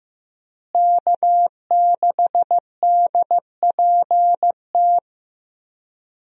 Morse code, heard as a keyed tone, about 15 wpm, 700 Hz. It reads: K6DPT
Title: USA Call Signs - 5 Characters - Repeated 2x - 15wpm